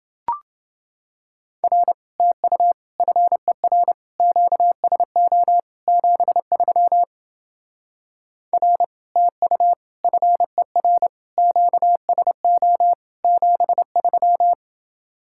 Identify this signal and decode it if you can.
Morse code, heard as a keyed tone, about 30 wpm, 700 Hz. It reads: R TU FER QSO 73 R TU FER QSO 73